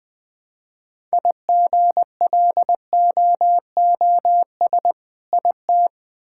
Morse code, heard as a keyed tone, about 21 wpm, 700 Hz. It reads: IGLOOS IT
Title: All-Age Appropriate Jokes - 20wpm